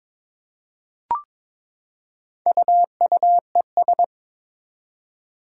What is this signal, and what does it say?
Morse code, heard as a keyed tone, about 22 wpm, 700 Hz. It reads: UUES
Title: Four Challenging Letter-Number Combinations 22wpm